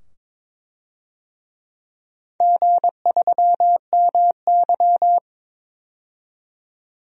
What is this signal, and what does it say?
Morse code, heard as a keyed tone, about 22 wpm, 700 Hz. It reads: G3MY
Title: DXCC-100 Call Signs - 4 Characters - 22wpm